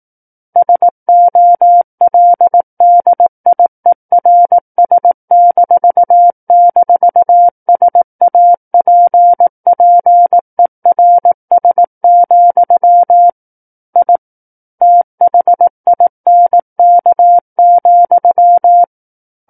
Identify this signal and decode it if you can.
Morse code, heard as a keyed tone, about 18 wpm, 700 Hz. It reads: SOLDIERS--SAPPERS, I THINK,
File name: 18 WPM